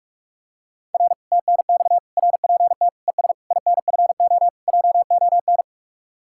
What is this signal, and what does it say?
Morse code, heard as a keyed tone, about 49 wpm, 700 Hz. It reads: R TNX RPT ES INFO JON